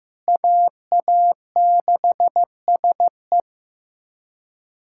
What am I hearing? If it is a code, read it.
Morse code, heard as a keyed tone, about 15 wpm, 700 Hz. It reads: AA6SE
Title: USA Callsign DE Callsign - 5 Characters - 15wpm